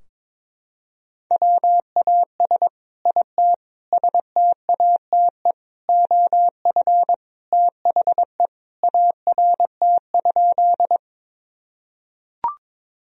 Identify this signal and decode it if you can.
Morse code, heard as a keyed tone, about 22 wpm, 700 Hz. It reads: WAS IT STATE OF THE ART?
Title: Sentences from Top 1000 Words - Review Part 3 - 22wpm